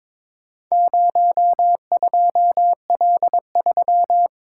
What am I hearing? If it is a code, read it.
Morse code, heard as a keyed tone, about 22 wpm, 700 Hz. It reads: 02L3
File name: practice / 22wpm / Four Letter-Number Combinations - 22wpm